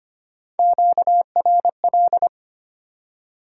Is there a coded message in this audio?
Yes: Morse code from 0.6 to 2.3 s, heard as a keyed tone, about 33 wpm, 700 Hz. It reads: QRL